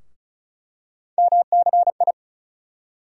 Morse code, heard as a keyed tone, about 35 wpm, 700 Hz. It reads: MCI
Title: USA Call Sign Postfixes - 3 Letters 35wpm